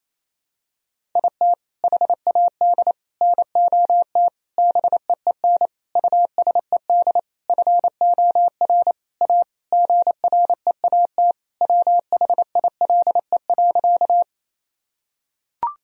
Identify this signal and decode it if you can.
Morse code, heard as a keyed tone, about 28 wpm, 700 Hz. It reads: IT HAD NOT BEEN USED FOR A GREAT WHILE.
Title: Sentences from Top 1000 Words - Review Part 1 - 28wpm